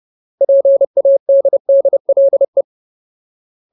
Morse code, heard as a keyed tone, about 30 wpm, 550 Hz. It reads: PADDLE
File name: Ultimate Word List / 30WPM Ultimate